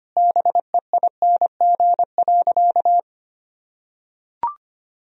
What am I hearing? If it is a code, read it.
Morse code, heard as a keyed tone, about 25 wpm, 700 Hz. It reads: BEING.
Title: Sentences from Top 1000 Words - Review Part 3 - 25wpm